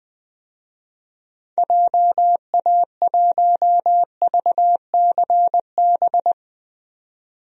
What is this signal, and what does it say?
Morse code, heard as a keyed tone, about 20 wpm, 700 Hz. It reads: JA1VCB
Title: DXCC-100 Call Signs - 6 Characters - 20wpm